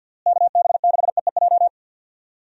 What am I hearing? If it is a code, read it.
Morse code, heard as a keyed tone, about 49 wpm, 700 Hz. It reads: KB6EEJ
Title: USA Call Signs - 6 Characters - Repeated 2x - 50wpm